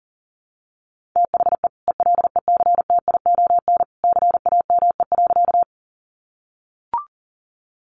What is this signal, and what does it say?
Morse code, heard as a keyed tone, about 40 wpm, 700 Hz. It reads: THE ELECTION CAME.
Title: Sentences from Top 1500 Words 40wpm